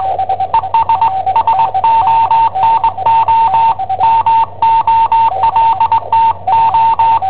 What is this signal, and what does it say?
Morse code, heard as a keyed tone, about 20 wpm, 910 Hz. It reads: ES SONO MOLTO
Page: Cassetta esercizi preparata da I7OHP per IW5ELP